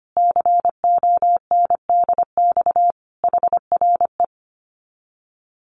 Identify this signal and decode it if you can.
Morse code, heard as a keyed tone, about 25 wpm, 700 Hz. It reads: CONDX HRE
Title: Common QSO Phrases 25wpm